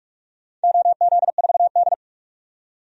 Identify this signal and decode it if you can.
Morse code, heard as a keyed tone, about 42 wpm, 700 Hz. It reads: OZ4D
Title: DXCC-100 Call Signs - 4 Characters - 45wpm